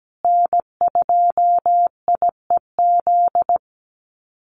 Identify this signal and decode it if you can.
Morse code, heard as a keyed tone, about 17 wpm, 700 Hz. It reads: N2IEZ
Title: USA Call Signs Repeated 3x 17wpm